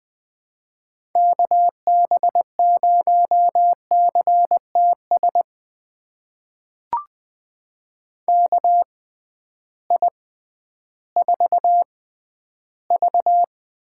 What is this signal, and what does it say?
Morse code, heard as a keyed tone, about 20 wpm, spaced as for 12 wpm, 700 Hz. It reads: KB0CTS KI4V
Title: USA Call Signs - 6 Characters - 6x Character Spacing - 20wpm